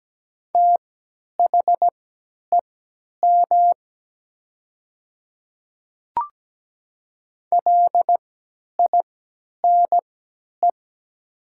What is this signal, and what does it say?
Morse code, heard as a keyed tone, about 17 wpm, spaced as for 10 wpm, 700 Hz. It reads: THEM LINE
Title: 3x Character Spacing - Top 200 Words - 17wpm